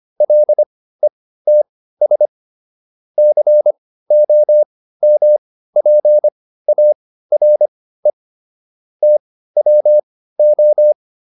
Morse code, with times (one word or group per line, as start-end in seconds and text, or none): 0.2-2.3 LETS
3.2-8.1 COMPARE
9.0-10.9 TWO